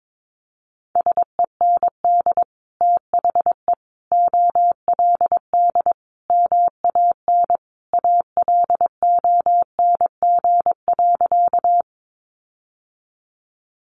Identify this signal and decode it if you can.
Morse code, heard as a keyed tone, about 22 wpm, 700 Hz. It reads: SEND THE OLD MAN ALONG.
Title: Sentences from Top 1500 Words 22wpm